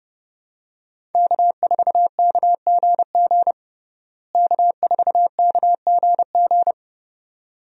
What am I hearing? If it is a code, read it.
Morse code, heard as a keyed tone, about 31 wpm, 700 Hz. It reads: K4KGG K4KGG
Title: USA Call Signs - 5 Characters - Repeated 2x - 30wpm